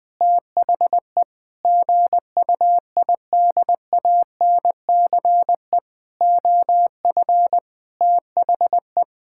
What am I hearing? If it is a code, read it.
Morse code, heard as a keyed tone, about 20 wpm, 700 Hz. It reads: THE GUIDANCE OF THE
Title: Sentences from Top 1600 Words 20wpm